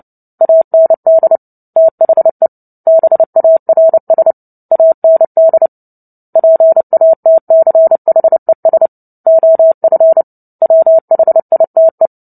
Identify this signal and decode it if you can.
Morse code, heard as a keyed tone, about 30 wpm, 650 Hz. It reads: AND THE BARS AND PATCHES OF WHITE
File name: Book Two / 03-Chapter 3 - 29 WPM